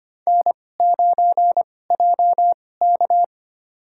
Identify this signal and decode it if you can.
Morse code, heard as a keyed tone, about 25 wpm, spaced as for 18 wpm, 700 Hz. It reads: N9JK
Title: USA Call Signs - 4 Characters - 2x Character Spacing - 25wpm